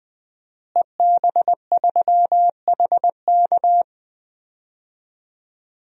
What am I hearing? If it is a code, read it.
Morse code, heard as a keyed tone, about 21 wpm, 700 Hz. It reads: EB3HK
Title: DXCC-100 Call Signs - 5 Characters - Repeated 2x - 20wpm